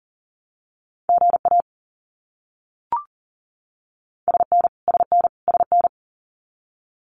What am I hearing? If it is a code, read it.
Morse code, heard as a keyed tone, about 40 wpm, 700 Hz. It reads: GA SN SN SN
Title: Abbreviations Repeated 3x - 40wpm